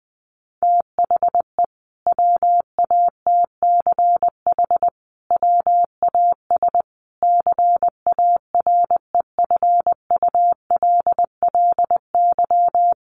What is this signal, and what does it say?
Morse code, heard as a keyed tone, about 20 wpm, 700 Hz. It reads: THE WATCH WAS CAREFULLY
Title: Sentences from Top 1300 Words 20wpm